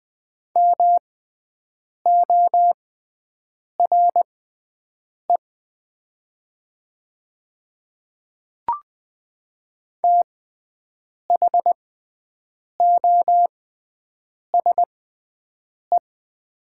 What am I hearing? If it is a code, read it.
Morse code, heard as a keyed tone, about 20 wpm, spaced as for 7 wpm, 700 Hz. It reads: MORE THOSE
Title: 6x Character Spacing - Top 300 Words - 20wpm